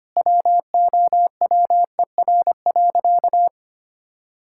Morse code, heard as a keyed tone, about 25 wpm, 700 Hz. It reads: WOWER.
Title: Pangram Challenge v2 - 25wpm